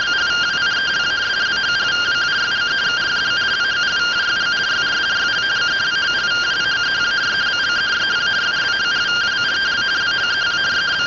Two-tone teleprinter signal, about 170 Hz shift.
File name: findings / axial iodine 10mm 184.gen-11 rtty rx